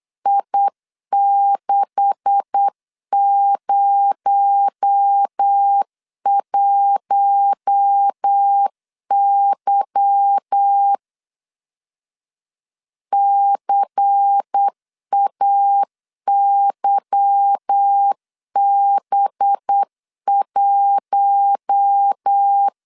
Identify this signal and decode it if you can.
Morse code, heard as a keyed tone, about 8 wpm, 800 Hz. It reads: I601Y CAYB1
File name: LEZIONI / 10 SIGNS / 10B INT 8wpm 3534